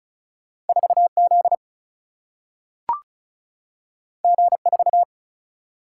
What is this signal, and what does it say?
Morse code, heard as a keyed tone, about 36 wpm, 700 Hz. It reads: 4 Z G 4
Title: Single-Letter Single-Number 35wpm